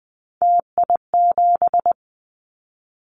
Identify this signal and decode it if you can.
Morse code, heard as a keyed tone, about 20 wpm, 700 Hz. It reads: TI7
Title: Callsign Prefixes for Top 100 DXCC - 20wpm